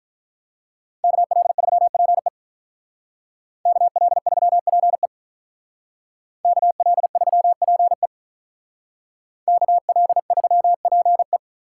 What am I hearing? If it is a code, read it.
Morse code, heard as a keyed tone, about 42 wpm, 700 Hz. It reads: KL3PE KL3PE KL3PE KL3PE
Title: Speed Racing - USA 2x2 Call Signs 35wpm